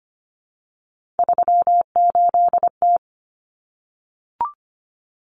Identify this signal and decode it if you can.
Morse code, heard as a keyed tone, about 25 wpm, 700 Hz. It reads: 38T
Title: Common Cut Numbers - 3-Digit 25wpm